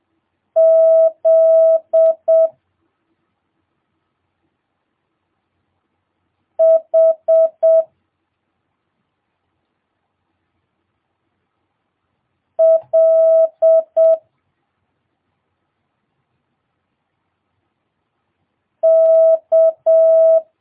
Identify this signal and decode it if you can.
Morse code, heard as a keyed tone, about 7 wpm, 650 Hz. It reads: Z H L K
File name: jh8cbh / 2022cw